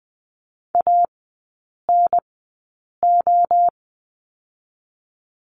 Morse code, heard as a keyed tone, about 20 wpm, 700 Hz. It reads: A N O
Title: 023 Lesson 5e - Letter o - Three Characters 20wpm